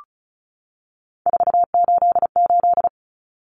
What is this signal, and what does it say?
Morse code, heard as a keyed tone, about 34 wpm, 700 Hz. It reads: 488
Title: Common Cut Numbers - 3-Digit 35wpm